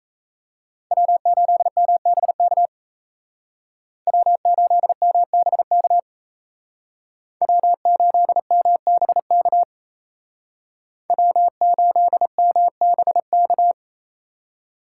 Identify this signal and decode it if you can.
Morse code, heard as a keyed tone, about 35 wpm, 700 Hz. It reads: W8MBK W8MBK W8MBK W8MBK